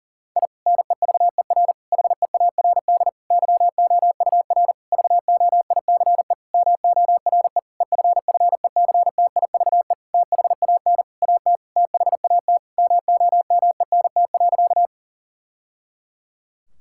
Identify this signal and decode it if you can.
Morse code, heard as a keyed tone, about 40 wpm, 700 Hz. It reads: I NEVER HEARD YOUR VOICE MORE EFFECTIVE THAN AT THAT MOMENT.
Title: Sentences from Top 1000 Words - Review Part 6 - 40wpm